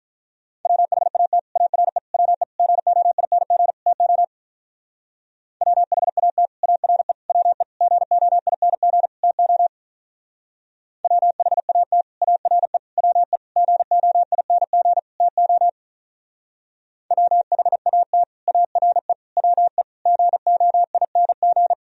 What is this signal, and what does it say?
Morse code, heard as a keyed tone, about 43 wpm, 700 Hz. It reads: WHAT ARE WE GOING TO WHAT ARE WE GOING TO WHAT ARE WE GOING TO WHAT ARE WE GOING
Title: Speed-Racing - Sets of 5 Words 35wpm